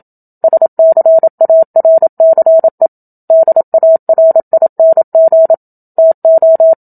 Morse code, heard as a keyed tone, about 27 wpm, 650 Hz. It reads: SCARCE DARING TO
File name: Book Two / 01-Chapter 1 - 27 WPM